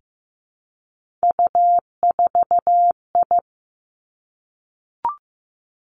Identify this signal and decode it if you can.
Morse code, heard as a keyed tone, about 15 wpm, 700 Hz. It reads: U4I
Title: Three Challenging Letter-Number Combinations 15wpm